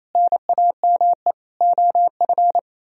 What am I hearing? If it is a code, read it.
Morse code, heard as a keyed tone, about 33 wpm, 700 Hz. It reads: NAME OF
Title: Speed-Racing - Sets of 3 Words 28wpm